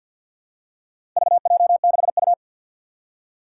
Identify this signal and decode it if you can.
Morse code, heard as a keyed tone, about 53 wpm, 700 Hz. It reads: VJ6U
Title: Four Challenging Letter-Number Combinations 50wpm